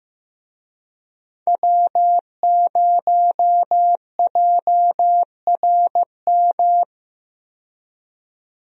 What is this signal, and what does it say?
Morse code, heard as a keyed tone, about 15 wpm, 700 Hz. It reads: W0JRM
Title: USA Call Signs - 5 Characters - 15wpm